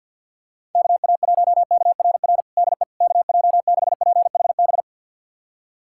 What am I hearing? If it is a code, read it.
Morse code, heard as a keyed tone, about 50 wpm, 700 Hz. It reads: KA1KRR DE KJ6PHB
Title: USA Callsign DE Callsign - 6 Characters - 50wpm